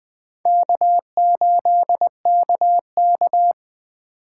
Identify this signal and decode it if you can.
Morse code, heard as a keyed tone, about 20 wpm, 700 Hz. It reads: K8KK
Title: Speed Racing - USA 1x2 Call Signs 17wpm